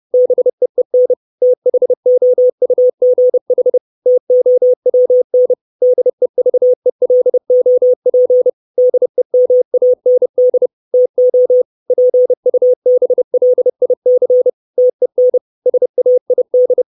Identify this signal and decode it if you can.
Morse code, heard as a keyed tone, about 30 wpm, 500 Hz. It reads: BEEN THOUGH TOWN DEVELOP DEMAND TO PUBLIC TEN SAID
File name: 632 Words 30WPM